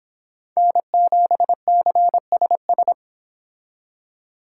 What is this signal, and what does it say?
Morse code, heard as a keyed tone, about 26 wpm, 700 Hz. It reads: N7CSS